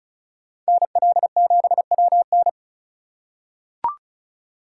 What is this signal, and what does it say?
Morse code, heard as a keyed tone, about 36 wpm, 700 Hz. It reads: NL7WN
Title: USA Call Signs - 3x Character Spacing - 35wpm